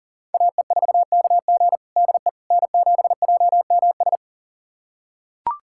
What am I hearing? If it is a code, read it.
Morse code, heard as a keyed tone, about 37 wpm, 700 Hz. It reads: AE4KG DE N7JMS
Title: USA Callsign DE Callsign - 5 Characters - 40wpm